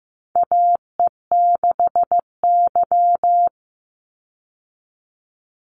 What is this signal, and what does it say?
Morse code, heard as a keyed tone, about 15 wpm, 700 Hz. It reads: AE6Y
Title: Call Signs Repeated 2x - 15wpm